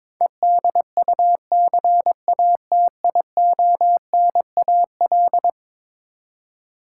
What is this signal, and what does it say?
Morse code, heard as a keyed tone, about 22 wpm, 700 Hz. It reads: EDUCATIONAL
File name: practice / 22wpm / Eleven Letter Words - 22wpm